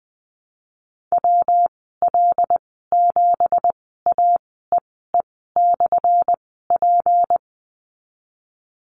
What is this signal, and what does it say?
Morse code, heard as a keyed tone, about 19 wpm, spaced as for 14 wpm, 700 Hz. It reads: WL7AEE/P
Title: 179 Lesson 29g - Slash - Call Signs 20wpm